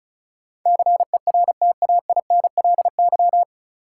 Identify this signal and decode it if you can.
Morse code, heard as a keyed tone, about 36 wpm, 700 Hz. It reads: CERTAINLY
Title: Sentences from Top 900 Words - New Word Drill - 35wpm